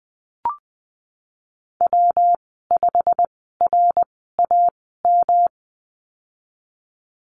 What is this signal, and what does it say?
Morse code, heard as a keyed tone, about 20 wpm, spaced as for 15 wpm, 700 Hz. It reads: W5RAM
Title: 128 Lesson 21g - Number 6 - Call Signs 20wpm